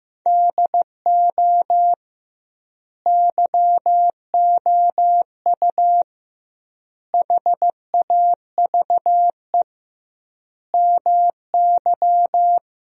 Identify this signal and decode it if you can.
Morse code, heard as a keyed tone, about 15 wpm, 700 Hz. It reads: DO YOU HAVE MY
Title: 2x Word-Spacing - Sentences from Top 500 Words - 15wpm